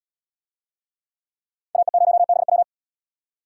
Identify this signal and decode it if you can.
Morse code, heard as a keyed tone, about 76 wpm, 700 Hz. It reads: AE0BW